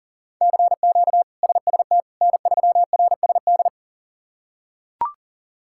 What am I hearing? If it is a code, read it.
Morse code, heard as a keyed tone, about 40 wpm, 700 Hz. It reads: CQ SST N3RSD
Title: K1USN SST QSO Practice - 40wpm